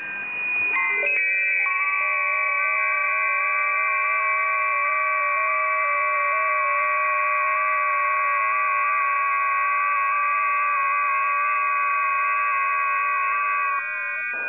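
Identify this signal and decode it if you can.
FT8.
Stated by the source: FT8 test recording received via WebSDR.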